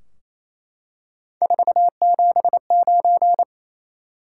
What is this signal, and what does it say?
Morse code, heard as a keyed tone, about 28 wpm, 700 Hz. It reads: 479
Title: Top 180 QSO Elements 28wpm